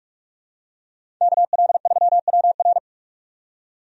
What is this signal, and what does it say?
Morse code, heard as a keyed tone, about 46 wpm, 700 Hz. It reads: KL3WR
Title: USA Call Signs - 5 Characters - 45wpm